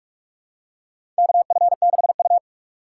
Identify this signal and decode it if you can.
Morse code, heard as a keyed tone, about 42 wpm, 700 Hz. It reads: KF6U
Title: USA Call Signs - 4 Characters - 6x Character Spacing - 45wpm